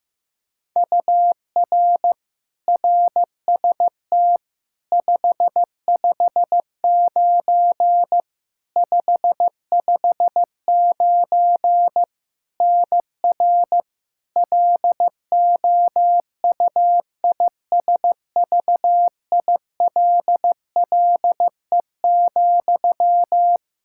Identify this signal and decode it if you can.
Morse code, heard as a keyed tone, about 16 wpm, 700 Hz. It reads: UR RST 559 559 NR LOUISVILLE,
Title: Practice QSO - Exchange 1 - Beginning - 15wpm